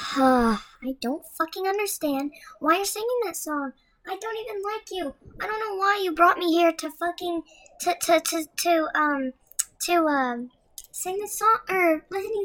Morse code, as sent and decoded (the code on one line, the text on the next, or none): none